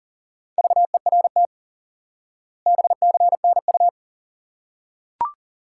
Morse code, heard as a keyed tone, about 40 wpm, 700 Hz. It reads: VERT BCNU
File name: IQE Course 3 - Lesson 89 - YAGI - Two Elements - 40wpm